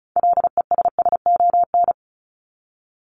Morse code, heard as a keyed tone, about 34 wpm, 700 Hz. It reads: LESSON
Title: Top 1001-1250 Words 35wpm